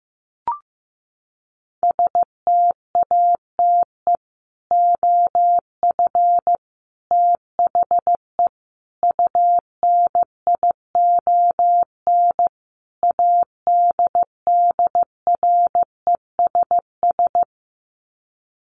Morse code, heard as a keyed tone, about 15 wpm, 700 Hz. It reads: STATE OF THE UNION ADDRESS
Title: Sets of 5 Words Encore 15wpm